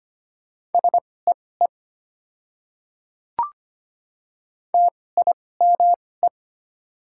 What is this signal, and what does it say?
Morse code, heard as a keyed tone, about 25 wpm, spaced as for 18 wpm, 700 Hz. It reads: SEE TIME
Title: 2x Character Spacing - Top 300 Words - 25wpm